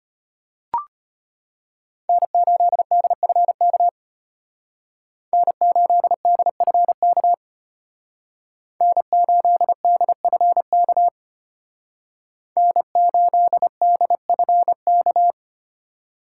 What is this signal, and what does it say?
Morse code, heard as a keyed tone, about 32 wpm, 700 Hz. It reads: N8DFK N8DFK N8DFK N8DFK